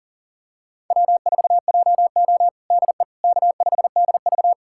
Morse code, heard as a keyed tone, about 40 wpm, 700 Hz. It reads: W4JO DE K5DV
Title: USA Callsign DE Callsign - 4 Characters - 40wpm